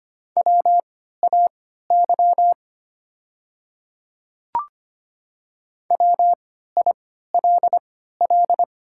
Morse code, heard as a keyed tone, about 25 wpm, spaced as for 14 wpm, 700 Hz. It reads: WAY WILL
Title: 3x Character Spacing - Top 100 Words - 25wpm